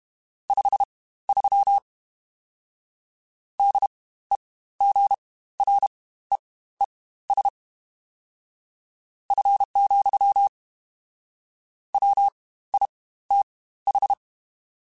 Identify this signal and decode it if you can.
Morse code, heard as a keyed tone, about 32 wpm, spaced as for 16 wpm, 800 Hz. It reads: 53 DEGREES F, WITH
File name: Spaced Characters / Spaced Characters-mp3s / Weather5 1 sp @30